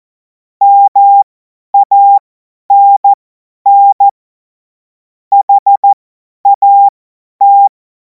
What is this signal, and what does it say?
Morse code, heard as a keyed tone, about 14 wpm, spaced as for 10 wpm, 800 Hz. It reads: MANN HAT
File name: Morsekurs / Witze / Witz 04. August 2026 50 F70